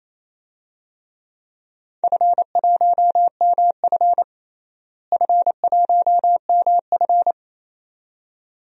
Morse code, heard as a keyed tone, about 28 wpm, 700 Hz. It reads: F1MF F1MF
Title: DXCC-100 Call Signs - 4 Characters - Repeated 2x - 28wpm